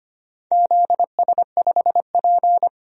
Morse code, heard as a keyed tone, about 25 wpm, 700 Hz. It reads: ZS5P
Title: DXCC-100 Call Signs - 4 Characters - 25wpm